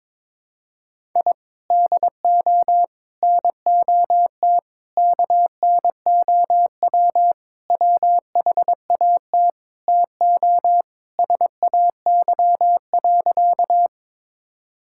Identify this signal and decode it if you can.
Morse code, heard as a keyed tone, about 22 wpm, 700 Hz. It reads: I DO NOT KNOW WHAT TO SAY.
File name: practice / 22wpm / Sentences with 4-Word Phrases - 22wpm